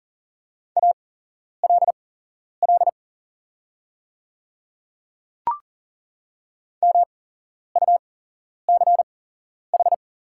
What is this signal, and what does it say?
Morse code, heard as a keyed tone, about 40 wpm, spaced as for 11 wpm, 690 Hz. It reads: ALL MUCH
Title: 8x Character Spacing - Top 200 Words - 40wpm